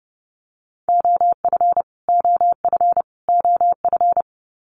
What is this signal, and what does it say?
Morse code, heard as a keyed tone, about 30 wpm, 700 Hz. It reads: OF OF OF
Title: Top 300 Words Repeated 30wpm